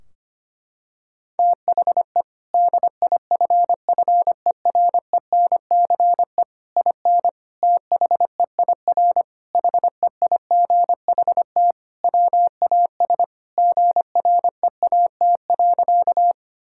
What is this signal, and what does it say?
Morse code, heard as a keyed tone, about 25 wpm, 700 Hz. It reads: THE DIFFERENCE IN THEIR HEIGHT WAS GREAT.
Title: Sentences from Top 1600 Words 25wpm